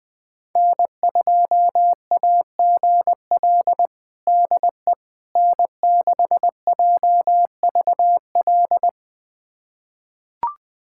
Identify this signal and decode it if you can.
Morse code, heard as a keyed tone, about 20 wpm, 700 Hz. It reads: N2AGL DE N6JVL
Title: USA Callsign DE Callsign - 5 Characters - 20wpm